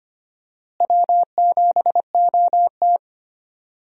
Morse code, heard as a keyed tone, about 25 wpm, 700 Hz. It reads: W7OT
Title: USA Call Signs - 4 Characters - Repeated 2x - 25wpm